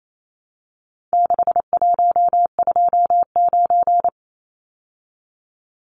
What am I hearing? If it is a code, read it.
Morse code, heard as a keyed tone, about 28 wpm, 700 Hz. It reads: 6129
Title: Numbers - 4-Digit 28wpm